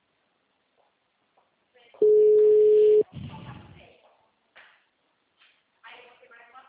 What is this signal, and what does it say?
Ringback tone (Europe, 425 Hz), with speech.
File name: out- -2022- - - .65783